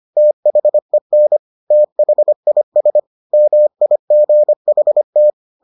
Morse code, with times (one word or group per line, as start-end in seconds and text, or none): 0.2-1.4 THEN
1.7-3.0 THIS
3.3-5.3 MIGHT